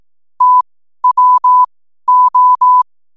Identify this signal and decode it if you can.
Morse code, heard as a keyed tone, about 18 wpm, spaced as for 12 wpm, 1000 Hz. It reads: TWO
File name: n7net / 12wpm